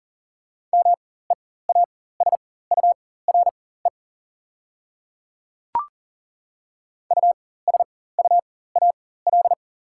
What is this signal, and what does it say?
Morse code, heard as a keyed tone, about 40 wpm, spaced as for 19 wpm, 700 Hz. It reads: MEASURE USUAL
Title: 4x Character Spacing - Top 500 Words - 40wpm